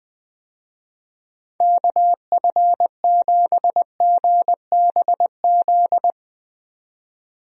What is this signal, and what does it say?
Morse code, heard as a keyed tone, about 20 wpm, 700 Hz. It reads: KF7GBZ
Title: USA Call Signs - 6 Characters - 20wpm